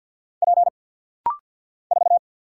Morse code, heard as a keyed tone, about 50 wpm, 700 Hz. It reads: P 4
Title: Single Letter-Number Alt - Rapid-Fire 50wpm